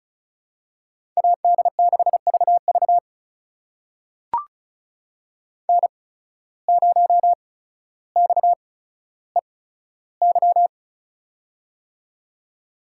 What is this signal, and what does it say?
Morse code, heard as a keyed tone, about 36 wpm, spaced as for 15 wpm, 700 Hz. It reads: AD6VV N0XEY
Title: USA Call Signs - 5 Characters - 8x Character Spacing - 35wpm